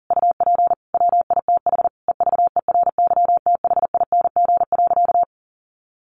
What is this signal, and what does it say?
Morse code, heard as a keyed tone, about 40 wpm, 700 Hz. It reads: UP WITH EVERYTHING.
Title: Sentences from Top 1100 Words 40wpm